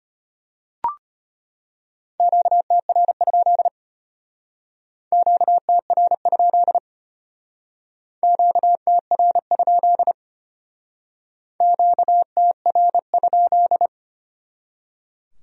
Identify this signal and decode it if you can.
Morse code, heard as a keyed tone, about 34 wpm, 700 Hz. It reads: QTR? QTR? QTR? QTR?